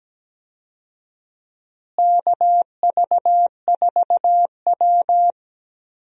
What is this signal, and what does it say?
Morse code, heard as a keyed tone, about 17 wpm, 700 Hz. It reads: KV4W
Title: USA Call Signs - 4 Characters - Repeated 3x - 17wpm